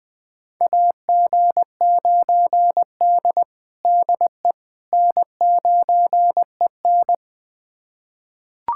Morse code, heard as a keyed tone, about 20 wpm, 700 Hz. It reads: AG9D DE N9EN
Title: USA Callsign DE Callsign - 4 Characters - 20wpm